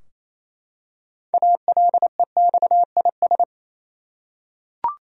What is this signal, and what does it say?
Morse code, heard as a keyed tone, about 28 wpm, 700 Hz. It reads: ALEXIS
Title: USA Common Names - 28wpm